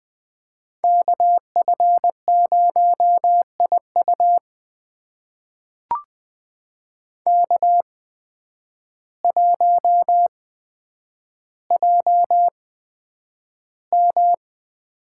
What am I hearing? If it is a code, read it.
Morse code, heard as a keyed tone, about 20 wpm, spaced as for 9 wpm, 700 Hz. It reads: KF0IU K1JM
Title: USA Call Signs - 8x Character Spacing - 20wpm